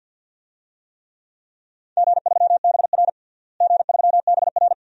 Morse code, heard as a keyed tone, about 52 wpm, 700 Hz. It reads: G3BR G3BR
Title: DXCC-100 Call Signs - 4 Characters - Repeated 2x - 50wpm